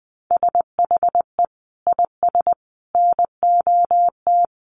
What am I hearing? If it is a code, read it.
Morse code, heard as a keyed tone, about 20 wpm, 700 Hz. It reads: SHE IS NOT
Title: Sentences from Top 1200 Words 20wpm